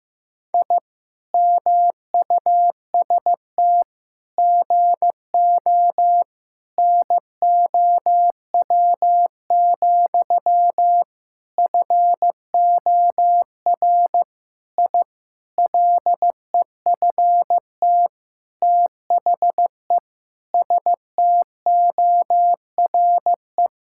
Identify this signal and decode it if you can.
Morse code, heard as a keyed tone, about 15 wpm, 700 Hz. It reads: I MUST GO NOW, FOR I LEFT THE STORE
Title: Sentences from Top 1000 Words - Review Part 5 - 15wpm